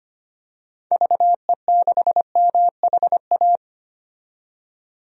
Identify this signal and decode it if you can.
Morse code, heard as a keyed tone, about 25 wpm, 700 Hz. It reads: VE6MHA
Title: DXCC-100 Call Signs - 6 Characters - 25wpm